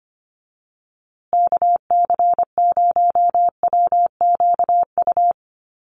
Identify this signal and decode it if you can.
Morse code, heard as a keyed tone, about 25 wpm, 700 Hz. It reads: KC0WQU
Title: USA Call Signs Repeated 2x 25wpm